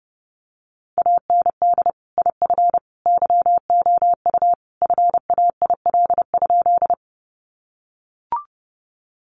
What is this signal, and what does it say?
Morse code, heard as a keyed tone, about 30 wpm, 700 Hz. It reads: AND IF YOU FAIL?
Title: Sentences from Top 1100 Words 30wpm